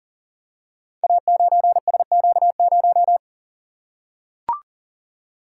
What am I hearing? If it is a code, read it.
Morse code, heard as a keyed tone, about 42 wpm, 700 Hz. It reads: A9SQ0
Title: Code Groups - 5 Letter:Numbers - 40wpm